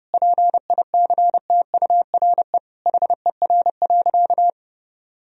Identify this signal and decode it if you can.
Morse code, heard as a keyed tone, about 30 wpm, 700 Hz. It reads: PICTURE HER.
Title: Sentences from Top 1000 Words - Review Part 1 - 30wpm